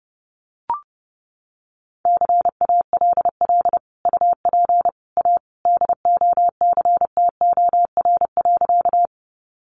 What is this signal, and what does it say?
Morse code, heard as a keyed tone, about 30 wpm, 700 Hz. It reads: CALL UP A DOCTOR.
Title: Sentences from Top 1500 Words 30wpm